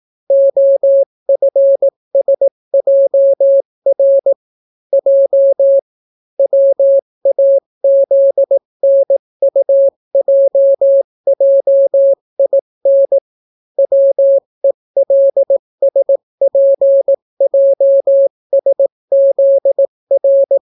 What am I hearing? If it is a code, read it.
Morse code, heard as a keyed tone, about 18 wpm, 550 Hz. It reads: OFSJR J WAZNUJJIN WELSPJSZR